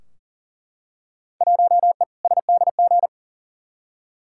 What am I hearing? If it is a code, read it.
Morse code, heard as a keyed tone, about 40 wpm, 700 Hz. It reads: 1E SDG
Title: Field Day Exchanges - 40wpm